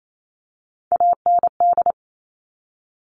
Morse code, heard as a keyed tone, about 28 wpm, 700 Hz. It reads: AND
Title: Top Three Letter Combinations in Words 28wpm